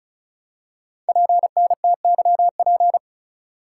Morse code, heard as a keyed tone, about 36 wpm, 700 Hz. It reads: PNTYP
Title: Code Groups - 5 Letter:Numbers - 35wpm